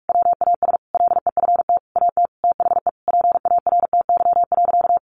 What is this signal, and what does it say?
Morse code, heard as a keyed tone, about 44 wpm, 700 Hz. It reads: WAS LEFT AT THE PARTY.
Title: Sentences from Top 1400 Words 45wpm